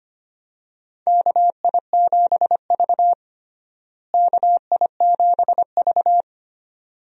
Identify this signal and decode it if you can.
Morse code, heard as a keyed tone, about 25 wpm, 700 Hz. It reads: KI7V KI7V
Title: USA Call Signs - 4 Characters - Repeated 2x - 25wpm